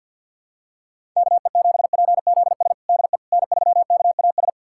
Morse code, heard as a keyed tone, about 53 wpm, 700 Hz. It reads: KE7PZS DE N3KAS
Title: USA Callsign DE Callsign 50wpm